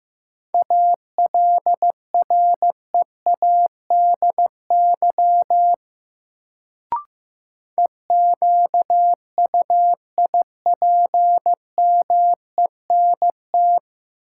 Morse code, heard as a keyed tone, about 15 wpm, 700 Hz. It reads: ALREADY EQUIPMENT
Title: Sentences from Top 700 Words - New Word Drill - 15wpm